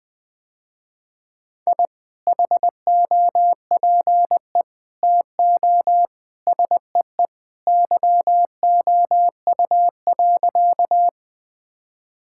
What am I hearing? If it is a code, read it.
Morse code, heard as a keyed tone, about 21 wpm, 700 Hz. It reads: I HOPE TO SEE YOU.
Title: Sentences with 2-Word Phrases - Spoken Beginning - 20wpm